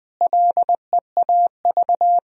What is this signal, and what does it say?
Morse code, heard as a keyed tone, about 21 wpm, 700 Hz. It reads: LEAV
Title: Sentences from Top 1000 Words - Review Part 2 - 20wpm